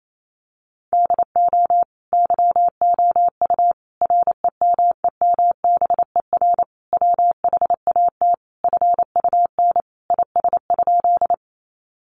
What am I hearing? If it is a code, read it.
Morse code, heard as a keyed tone, about 28 wpm, 700 Hz. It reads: DO YOU REMEMBER WHAT FUN IS?
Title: Sentences from Top 700 Words 28wpm